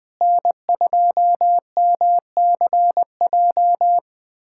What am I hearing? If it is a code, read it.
Morse code, heard as a keyed tone, about 21 wpm, 700 Hz. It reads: N2MCJ
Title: USA Callsign DE Callsign 20wpm